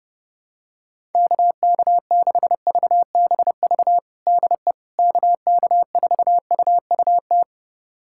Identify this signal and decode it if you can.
Morse code, heard as a keyed tone, about 30 wpm, 700 Hz. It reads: KK6VBV DE KK4UUT
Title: USA Callsign DE Callsign - 6 Characters - 30wpm